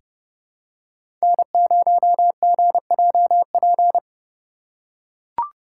Morse code, heard as a keyed tone, about 30 wpm, 700 Hz. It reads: N0GJP
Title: USA Call Signs - 5 Characters - 3x Character Spacing - 30wpm